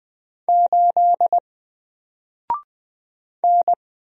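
Morse code, heard as a keyed tone, about 20 wpm, 700 Hz. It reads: 8 N
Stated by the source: Single letter and number Morse drill at 20 wpm, from N4LCD.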